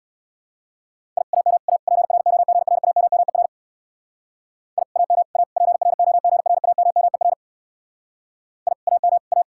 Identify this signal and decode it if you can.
Morse code, heard as a keyed tone, about 64 wpm, 700 Hz. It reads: I AM A PROGRAMMER I AM A PROGRAMMER I AM A